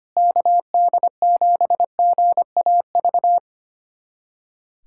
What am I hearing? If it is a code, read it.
Morse code, heard as a keyed tone, about 25 wpm, 700 Hz. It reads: KD7GAV
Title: USA Callsign DE Callsign - 6 Characters - 25wpm